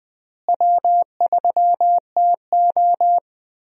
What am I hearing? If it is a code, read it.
Morse code, heard as a keyed tone, about 20 wpm, 700 Hz. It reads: W3TO
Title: USA Call Signs - 4 Characters - Repeated 3x - 20wpm